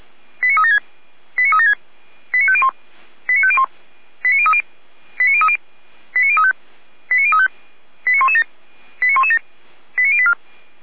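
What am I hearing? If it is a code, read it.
Five-tone selective call (ZVEI), heard as quick runs of five tones, tones of about 80 ms. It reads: 89376 89376 89501 89501 89030 89030 89035 89035 89117 89117 89084
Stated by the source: ZVEI five-tone selective call received on SDR.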